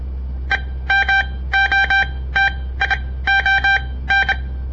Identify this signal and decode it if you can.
Morse code, heard as a keyed tone, about 25 wpm, spaced as for 17 wpm, 1550 Hz. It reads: EMOTION
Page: On the air Morse code practice from the Southern Montana Amatuer Radio Association; all code, no read back